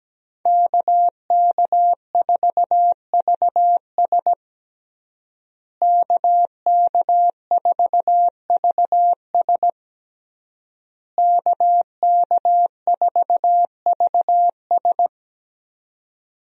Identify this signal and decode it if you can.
Morse code, heard as a keyed tone, about 17 wpm, 700 Hz. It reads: KK4VS KK4VS KK4VS
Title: USA Call Signs - 5 Characters - Repeated 3x - 17wpm